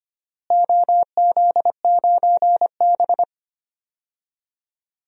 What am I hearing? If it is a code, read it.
Morse code, heard as a keyed tone, about 25 wpm, 700 Hz. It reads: OZ9B
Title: DXCC-100 Call Signs - 4 Characters - Repeated 2x - 25wpm